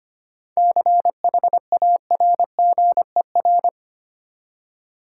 Morse code, heard as a keyed tone, about 25 wpm, 690 Hz. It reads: CHARGER
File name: practice / 25wpm / Easy Words with Hints - 25wpm